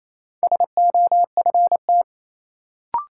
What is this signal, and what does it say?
Morse code, heard as a keyed tone, about 27 wpm, 700 Hz. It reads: SOFT
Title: Sentences from Top 1500 Words - New Word Drill Alt - 28wpm